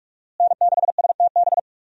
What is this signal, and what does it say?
Morse code, heard as a keyed tone, about 46 wpm, 700 Hz. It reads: N6STB
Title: USA Call Signs - 5 Characters - Repeated 2x - 45wpm